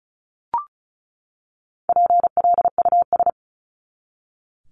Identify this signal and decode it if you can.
Morse code, heard as a keyed tone, about 34 wpm, 700 Hz. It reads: PLUS
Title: Top 1001-1250 Words 35wpm